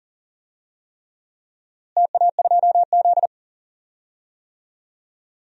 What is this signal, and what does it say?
Morse code, heard as a keyed tone, about 37 wpm, 700 Hz. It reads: TA2Z
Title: DXCC-100 Call Signs - 4 Characters - Repeated 3x - 40wpm